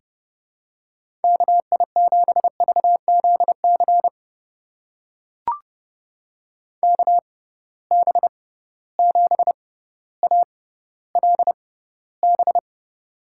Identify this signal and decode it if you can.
Morse code, heard as a keyed tone, about 30 wpm, spaced as for 15 wpm, 700 Hz. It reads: KI7VZC KB7ALB
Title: USA Call Signs - 6 Characters - 6x Character Spacing - 30wpm